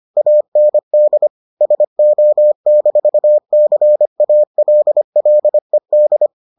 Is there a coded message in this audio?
Yes: Morse code from 0.2 to 6.3 s, heard as a keyed tone, about 25 wpm, 600 Hz. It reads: AND SO-CALLED